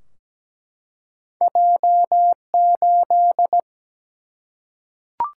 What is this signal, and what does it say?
Morse code, heard as a keyed tone, about 17 wpm, 700 Hz. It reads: J8
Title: Single-Letter Single-Number 17wpm